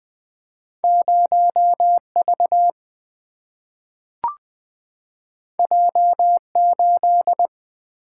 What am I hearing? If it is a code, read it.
Morse code, heard as a keyed tone, about 20 wpm, 700 Hz. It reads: 0V J8
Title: Single-Letter Single-Number 20wpm